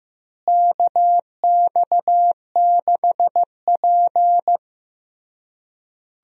Morse code, heard as a keyed tone, about 14 wpm, 700 Hz. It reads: KX6P
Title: USA Call Signs - 4 Characters - Repeated 3x - 15wpm